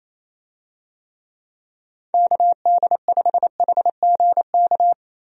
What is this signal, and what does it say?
Morse code, heard as a keyed tone, about 28 wpm, 700 Hz. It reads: KD5HGK
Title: USA Call Signs - 6 Characters - Repeated 3x - 28wpm